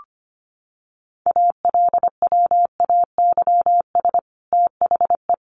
Morse code, heard as a keyed tone, about 25 wpm, 700 Hz. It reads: ALWAYS THE
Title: Sentences from Top 300 Words 25wpm